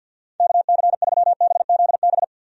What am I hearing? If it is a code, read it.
Morse code, heard as a keyed tone, about 53 wpm, 700 Hz. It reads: KC3BZB
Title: USA Call Signs - 6 Characters - 50wpm